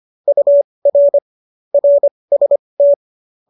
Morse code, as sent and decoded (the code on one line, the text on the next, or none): ..- .-. / .-. ... -
UR RST